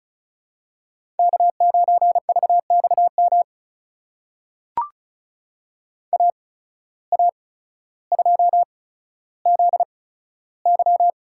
Morse code, heard as a keyed tone, about 34 wpm, spaced as for 15 wpm, 700 Hz. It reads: K9VXM AA2ZY
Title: USA Call Signs - 5 Characters - 8x Character Spacing - 35wpm